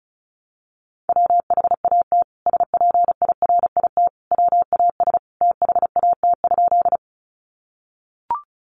Morse code, heard as a keyed tone, about 34 wpm, 700 Hz. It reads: WHAT SPIRIT WAS THAT?
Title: Sentences from Top 1400 Words 35wpm